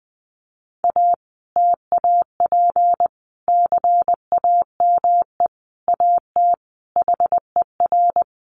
Morse code, heard as a keyed tone, about 20 wpm, 700 Hz. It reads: A TAP CAME AT HER
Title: Sentences from Top 1400 Words 20wpm